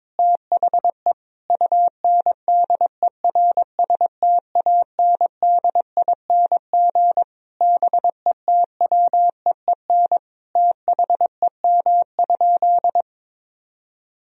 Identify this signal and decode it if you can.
Morse code, heard as a keyed tone, about 22 wpm, 700 Hz. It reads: THE UNDERSTANDING BETWEEN THEM?
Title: Sentences from Top 1000 Words - Review Part 3 - 22wpm